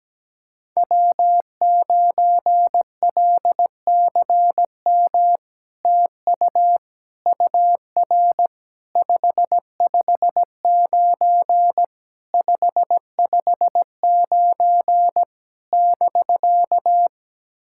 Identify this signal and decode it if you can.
Morse code, heard as a keyed tone, about 17 wpm, 700 Hz. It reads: W9LCM TU UR 559 559 <BK>
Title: SOTA QSO Practice 17wpm